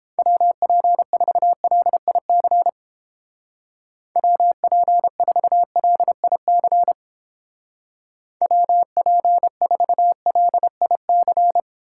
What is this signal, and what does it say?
Morse code, heard as a keyed tone, about 29 wpm, 700 Hz. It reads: WP4LIC WP4LIC WP4LIC